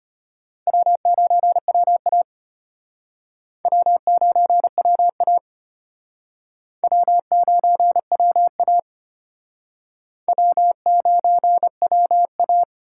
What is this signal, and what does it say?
Morse code, heard as a keyed tone, about 32 wpm, 700 Hz. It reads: W9WA W9WA W9WA W9WA